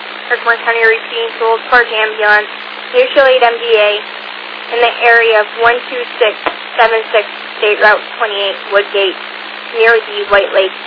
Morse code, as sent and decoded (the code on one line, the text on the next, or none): none